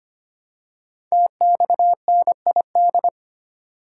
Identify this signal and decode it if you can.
Morse code, heard as a keyed tone, about 25 wpm, 700 Hz. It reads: TXNID
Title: Code Groups - 5 Letter:Numbers - 25wpm